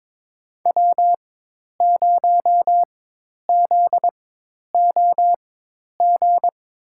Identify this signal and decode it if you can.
Morse code, heard as a keyed tone, about 21 wpm, spaced as for 10 wpm, 700 Hz. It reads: W0ZOG
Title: USA Call Signs - 5 Characters - 4x Character Spacing - 22wpm